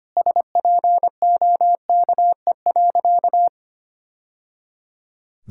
Morse code, heard as a keyed tone, about 25 wpm, 700 Hz. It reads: SPOKE.
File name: practice / 25wpm / Sentences with 3-Word Phrases - 25wpm